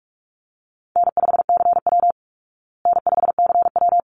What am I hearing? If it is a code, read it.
Morse code, heard as a keyed tone, about 44 wpm, 700 Hz. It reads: N5CW N5CW
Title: Easy Call Signs Repeated 2x - 45wpm